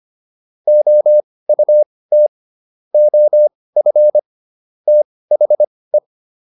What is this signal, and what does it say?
Morse code, heard as a keyed tone, about 25 wpm, spaced as for 18 wpm, 600 Hz. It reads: OUT OF THE
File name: WD204 18